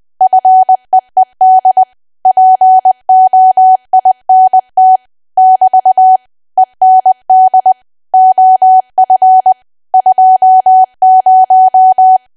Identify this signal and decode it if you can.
Morse code, heard as a keyed tone, about 20 wpm, 750 Hz. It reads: FEED POINT <BT> END OF 20
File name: cwa / W1AW-Level3 / 20WPM